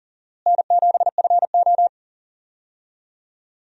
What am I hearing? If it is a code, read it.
Morse code, heard as a keyed tone, about 40 wpm, 700 Hz. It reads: N7FO
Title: USA Call Signs - 4 Characters - Repeated 3x - 40wpm